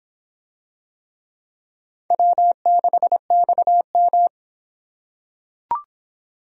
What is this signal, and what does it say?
Morse code, heard as a keyed tone, about 25 wpm, 700 Hz. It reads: W6XM